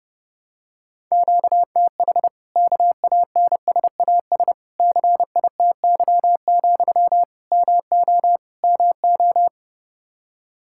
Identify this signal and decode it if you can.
Morse code, heard as a keyed tone, about 30 wpm, 700 Hz. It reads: QTH KANSAS CITY, MO MO
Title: Practice QSO - Exchange 1 - Beginning - 30wpm